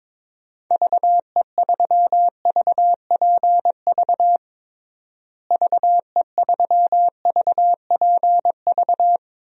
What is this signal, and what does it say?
Morse code, heard as a keyed tone, about 22 wpm, 700 Hz. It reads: VE3VPV VE3VPV
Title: DXCC-100 Call Signs - 6 Characters - Repeated 3x - 22wpm